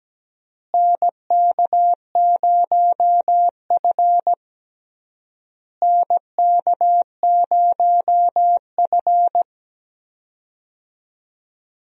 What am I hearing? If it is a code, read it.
Morse code, heard as a keyed tone, about 17 wpm, 700 Hz. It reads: NK0F NK0F
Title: USA Call Signs - 4 Characters - Repeated 3x - 17wpm